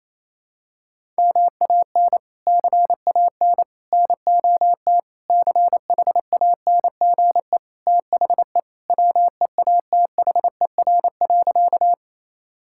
Morse code, heard as a keyed tone, about 28 wpm, 700 Hz. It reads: MAN CAN NOT CHANGE THE WEATHER.
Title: Sentences from Top 1000 Words - Review Part 5 - 28wpm